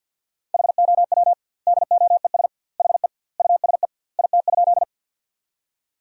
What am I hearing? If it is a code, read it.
Morse code, heard as a keyed tone, about 50 wpm, 700 Hz. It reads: HOW DOES HE USE IT?
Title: 2x Word-Spacing - Sentences from Top 200 Words - 50wpm